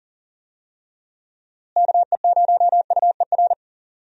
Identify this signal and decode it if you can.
Morse code, heard as a keyed tone, about 40 wpm, 700 Hz. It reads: KE0UER
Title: USA Call Signs - 6 Characters - Repeated 3x - 40wpm